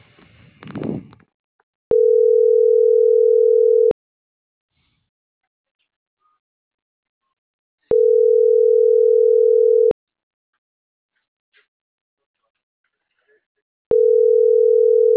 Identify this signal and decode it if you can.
Ringback tone (North America), 440 + 480 Hz.